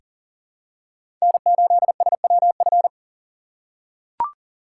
Morse code, heard as a keyed tone, about 40 wpm, 690 Hz. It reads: N8SWF
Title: USA Call Signs - 5 Characters - 3x Character Spacing - 40wpm